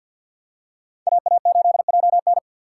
Morse code, heard as a keyed tone, about 54 wpm, 700 Hz. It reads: AA8JN
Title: USA Call Signs - 5 Characters - 6x Character Spacing - 50wpm